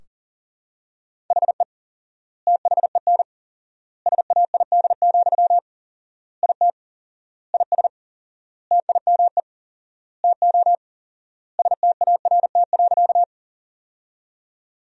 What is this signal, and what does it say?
Morse code, heard as a keyed tone, about 40 wpm, 700 Hz. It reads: HE THEN SAID, IT IS TIME TO START.
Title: 4x Word-Spacing - Sentences from Top 300 Words - 40wpm